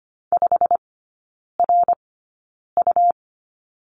Morse code, heard as a keyed tone, about 25 wpm, spaced as for 9 wpm, 700 Hz. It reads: 5RU
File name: Lesson 17e - Letter u - Three Characters 25wpm